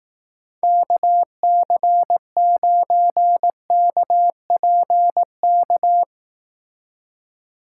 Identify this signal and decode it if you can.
Morse code, heard as a keyed tone, about 18 wpm, 700 Hz. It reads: KC9KPK
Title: Speed Racing - USA Call Signs 15wpm